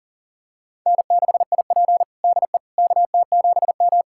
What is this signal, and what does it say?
Morse code, heard as a keyed tone, about 40 wpm, 700 Hz. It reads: N6IP DE KT7M
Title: USA Callsign DE Callsign - 4 Characters - 40wpm